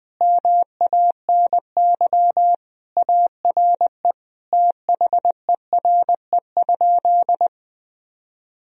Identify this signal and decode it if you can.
Morse code, heard as a keyed tone, about 20 wpm, 700 Hz. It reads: MANY ARE THERE?
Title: Sentences from Top 1000 Words - Review Part 1 - 20wpm